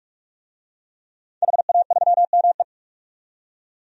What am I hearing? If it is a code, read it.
Morse code, heard as a keyed tone, about 47 wpm, 700 Hz. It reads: HA3ME
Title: DXCC-100 Call Signs - 5 Characters - 45wpm